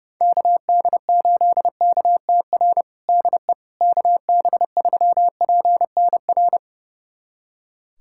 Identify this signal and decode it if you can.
Morse code, heard as a keyed tone, about 30 wpm, 700 Hz. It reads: KD8KTR DE KB3PNR
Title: USA Callsign DE Callsign - 6 Characters - 30wpm